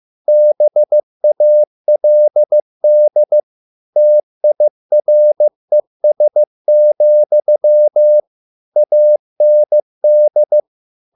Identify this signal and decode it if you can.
Morse code, heard as a keyed tone, about 15 wpm, 600 Hz. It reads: BALD TIRES, AND